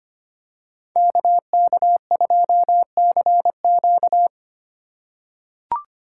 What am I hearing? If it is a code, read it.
Morse code, heard as a keyed tone, about 25 wpm, 700 Hz. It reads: KK2CQ
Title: USA Call Signs - 5 Characters - 3x Character Spacing - 25wpm